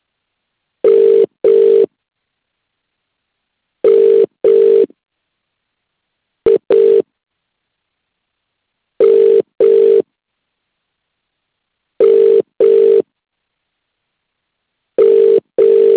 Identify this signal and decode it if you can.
Ringback tone (UK), 400 + 450 Hz.